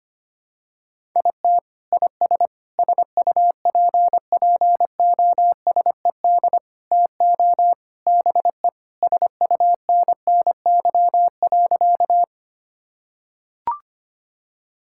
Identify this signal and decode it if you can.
Morse code, heard as a keyed tone, about 25 wpm, 700 Hz. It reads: IT IS SUPPOSED TO BE SUNNY.
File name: practice / 25wpm / Sentences with 3-Word Phrases - 25wpm